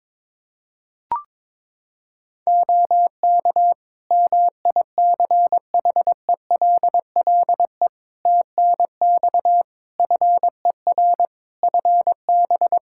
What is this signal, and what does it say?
Morse code, heard as a keyed tone, about 22 wpm, 700 Hz. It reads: OK MICHELLE TNX FER FB
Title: Practice QSO - Exchange 4 - Ending - 22wpm